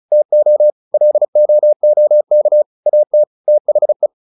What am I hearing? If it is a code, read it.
Morse code, heard as a keyed tone, about 34 wpm, 600 Hz. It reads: TO LOOK AT THE